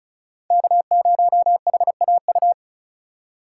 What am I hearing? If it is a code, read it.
Morse code, heard as a keyed tone, about 36 wpm, 700 Hz. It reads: K0HAU
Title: USA Call Signs - 5 Characters - 4x Character Spacing - 35wpm